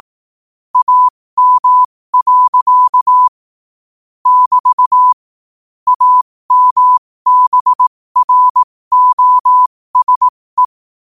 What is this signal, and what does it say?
Morse code, heard as a keyed tone, about 18 wpm, spaced as for 15 wpm, 1000 Hz. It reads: AM. <BT> AMBROSE
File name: 15wpm